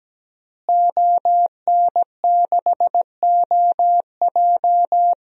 Morse code, heard as a keyed tone, about 17 wpm, 700 Hz. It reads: ON6OJ
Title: DXCC-100 Call Signs - 5 Characters - Repeated 3x - 17wpm